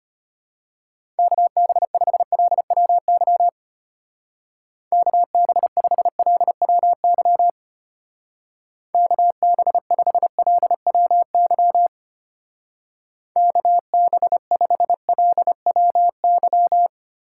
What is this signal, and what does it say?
Morse code, heard as a keyed tone, about 31 wpm, 700 Hz. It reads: KB5LWY KB5LWY KB5LWY KB5LWY
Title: Speed Racing - USA Call Signs 25wpm